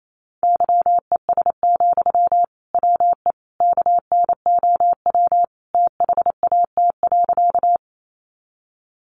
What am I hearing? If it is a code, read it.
Morse code, heard as a keyed tone, about 28 wpm, 700 Hz. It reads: YES, WE KNOW THAT.
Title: Sentences from Top 500 Words 28wpm